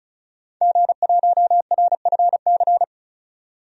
Morse code, heard as a keyed tone, about 34 wpm, 700 Hz. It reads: G1RFC
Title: DXCC-100 Call Signs - 5 Characters - 35wpm